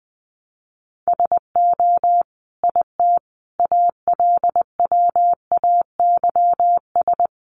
Morse code, heard as a keyed tone, about 20 wpm, 700 Hz. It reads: SO IT ALWAYS
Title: Sentences from Top 300 Words 20wpm